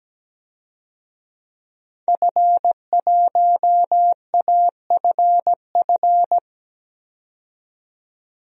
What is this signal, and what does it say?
Morse code, heard as a keyed tone, about 17 wpm, 700 Hz. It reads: F1AFF
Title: DXCC-100 Call Signs - 5 Characters - 17wpm